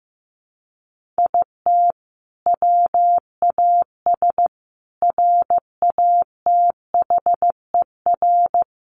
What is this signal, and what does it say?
Morse code, heard as a keyed tone, about 15 wpm, 700 Hz. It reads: IT WAS RATHER
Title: Sentences from Top 1400 Words 15wpm